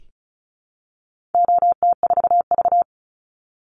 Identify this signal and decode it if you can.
Morse code, heard as a keyed tone, about 34 wpm, 700 Hz. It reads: OT4V
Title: Call Signs Repeated 2x - 35wpm